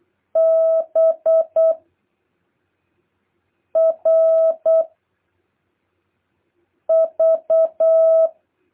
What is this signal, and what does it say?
Morse code, heard as a keyed tone, about 8 wpm, 650 Hz. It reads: B R V
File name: jh8cbh / 2023cw